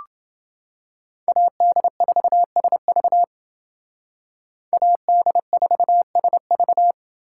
Morse code, heard as a keyed tone, about 29 wpm, 700 Hz. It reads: AD4SV AD4SV